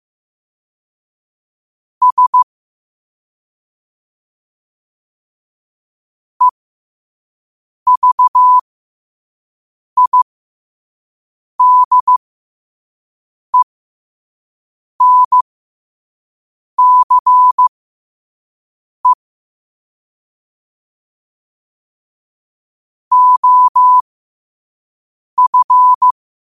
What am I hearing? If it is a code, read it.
Morse code, heard as a keyed tone, about 15 wpm, spaced as for 5 wpm, 1000 Hz. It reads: S EVIDENCE OF
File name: qotdfiles / 5wpm